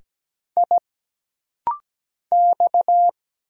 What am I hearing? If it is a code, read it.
Morse code, heard as a keyed tone, about 17 wpm, 700 Hz. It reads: I X
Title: Single Letters - Rapid-Fire 17wpm